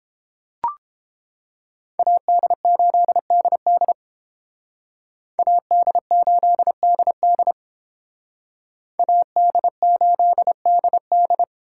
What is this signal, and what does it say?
Morse code, heard as a keyed tone, about 28 wpm, 700 Hz. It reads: AD8DD AD8DD AD8DD